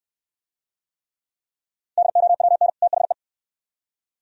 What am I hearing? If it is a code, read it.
Morse code, heard as a keyed tone, about 64 wpm, 700 Hz. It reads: DOWN THE